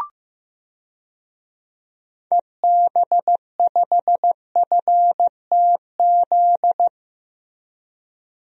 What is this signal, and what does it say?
Morse code, heard as a keyed tone, about 15 wpm, 700 Hz. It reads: EB5FTZ
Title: DXCC-100 Call Signs - 6 Characters - Repeated 2x - 15wpm